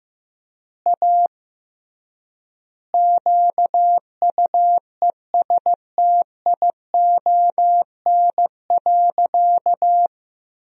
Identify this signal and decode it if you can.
Morse code, heard as a keyed tone, about 15 wpm, 700 Hz. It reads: A QUESTION.
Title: 3x Word-Spacing - Sentences from Top 400 Words - 15wpm